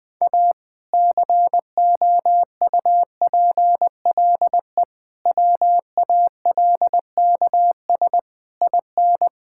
Morse code, heard as a keyed tone, about 20 wpm, 700 Hz. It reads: A COUPLE WALKS IN
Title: Sentences with 2-Word Phrases - Spoken Beginning - 20wpm